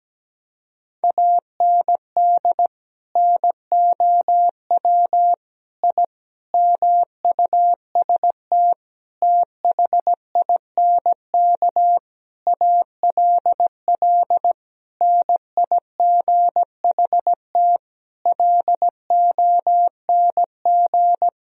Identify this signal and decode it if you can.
Morse code, heard as a keyed tone, about 17 wpm, 700 Hz. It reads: AND NOW I MUST THINK ALL NIGHT LONG
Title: Sentences from Top 1000 Words - Review Part 3 - 17wpm